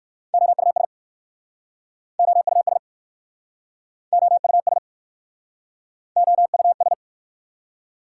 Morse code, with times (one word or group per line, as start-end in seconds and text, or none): 0.3-0.9 OUS
2.2-2.8 OUS
4.1-4.8 OUS
6.2-6.9 OUS